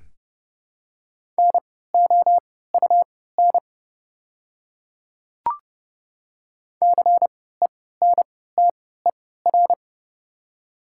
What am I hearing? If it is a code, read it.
Morse code, heard as a keyed tone, about 31 wpm, spaced as for 17 wpm, 700 Hz. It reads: NOUN CENTER
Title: 3x Character Spacing - Top 500 Words - 30wpm